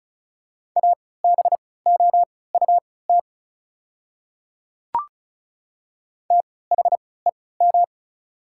Morse code, heard as a keyed tone, about 34 wpm, spaced as for 20 wpm, 690 Hz. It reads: ABOUT THEM
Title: 3x Character Spacing - Top 100 Words - 35wpm